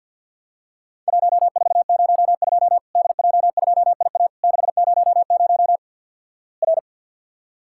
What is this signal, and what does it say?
Morse code, heard as a keyed tone, about 50 wpm, 700 Hz. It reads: 1402 DJ2IA 600 R
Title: Worked All Europe DX Contest - QTC Exchange with Repeats - 50wpm